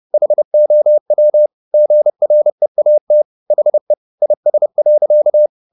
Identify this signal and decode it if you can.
Morse code, heard as a keyed tone, about 30 wpm, 600 Hz. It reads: HOW GREAT HE IS.
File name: the old man and the sea / the old man and the sea 1 30wpm